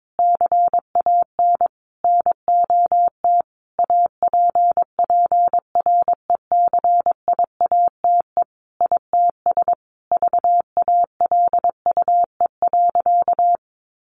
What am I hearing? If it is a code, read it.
Morse code, heard as a keyed tone, about 22 wpm, 700 Hz. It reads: CAN NOT APPRECIATE ITS VALUE.
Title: Sentences from Top 1300 Words 22wpm